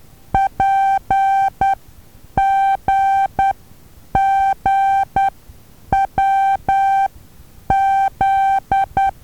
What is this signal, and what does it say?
Morse code, heard as a keyed tone, about 10 wpm, spaced as for 8 wpm, 780 Hz. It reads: PGGWZ
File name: Taviro HA6VA B